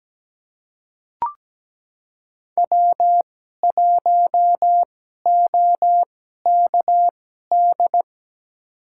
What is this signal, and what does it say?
Morse code, heard as a keyed tone, about 17 wpm, spaced as for 12 wpm, 700 Hz. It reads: W1OKD
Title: USA Call Signs - 5 Characters - 2x Character Spacing - 17wpm